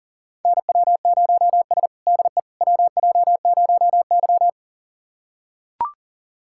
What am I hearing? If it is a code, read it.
Morse code, heard as a keyed tone, about 40 wpm, 690 Hz. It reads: NW0S DE WJ0Y
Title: USA Callsign DE Callsign - 4 Characters - 40wpm